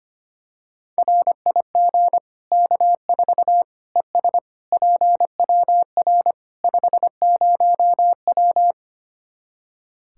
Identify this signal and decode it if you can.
Morse code, heard as a keyed tone, about 25 wpm, 700 Hz. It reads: RIG K4 ES PWR 50W
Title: K1USN SST QSO Practice 25wpm